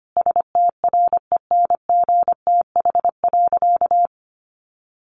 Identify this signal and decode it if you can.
Morse code, heard as a keyed tone, about 25 wpm, 700 Hz. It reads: STRENGTH.
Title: Sentences from Top 1300 Words 25wpm